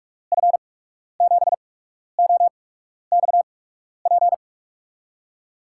Morse code, heard as a keyed tone, about 44 wpm, spaced as for 12 wpm, 700 Hz. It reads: F7OXP
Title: USA Call Signs - 8x Character Spacing - 45wpm